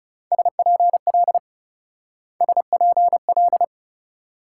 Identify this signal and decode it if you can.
Morse code, heard as a keyed tone, about 39 wpm, 700 Hz. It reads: SPL SPL